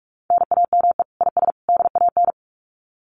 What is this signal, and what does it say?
Morse code, heard as a keyed tone, about 44 wpm, 700 Hz. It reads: NAME IS DAN
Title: Common QSO Phrases 45wpm